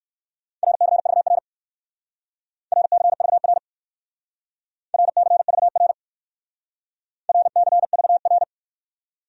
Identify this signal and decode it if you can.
Morse code, heard as a keyed tone, about 58 wpm, 700 Hz. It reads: RCVR RCVR RCVR RCVR